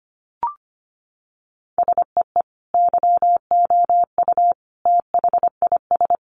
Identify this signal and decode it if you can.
Morse code, heard as a keyed tone, about 25 wpm, 700 Hz. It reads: SEE YOU THIS
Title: Sentences from Top 1500 Words 25wpm